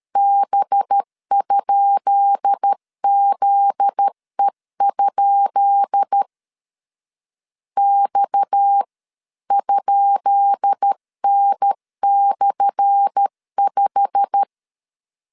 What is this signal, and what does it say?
Morse code, heard as a keyed tone, about 13 wpm, 800 Hz. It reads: B?ZE? X?N/5
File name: LEZIONI / 10 SIGNS / 10B INT 12wpm 2414